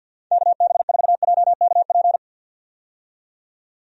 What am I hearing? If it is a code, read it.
Morse code, heard as a keyed tone, about 49 wpm, 700 Hz. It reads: KB4JKP
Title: USA Call Signs - 6 Characters - 50wpm